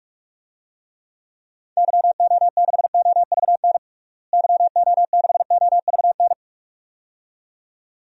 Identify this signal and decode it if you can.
Morse code, heard as a keyed tone, about 46 wpm, 700 Hz. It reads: YO6OVN YO6OVN
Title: DXCC-100 Call Signs - 6 Characters - Repeated 2x - 45wpm